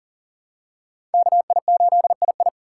Morse code, heard as a keyed tone, about 38 wpm, 700 Hz. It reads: KI8II
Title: USA Call Signs - 5 Characters - 3x Character Spacing - 40wpm